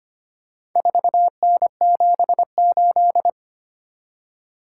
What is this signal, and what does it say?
Morse code, heard as a keyed tone, about 25 wpm, 700 Hz. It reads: 4N78
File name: practice / 25wpm / Four Letter-Number Combinations - 25wpm